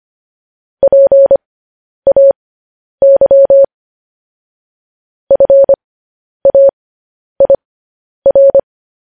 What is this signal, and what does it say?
Morse code, heard as a keyed tone, about 25 wpm, spaced as for 10 wpm, 550 Hz. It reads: PAY FAIR
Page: news headlines in morse code at 10 wpm